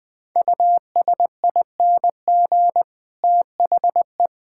Morse code, heard as a keyed tone, about 20 wpm, 700 Hz. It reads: USING THE
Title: All-Age Appropriate Jokes - 20wpm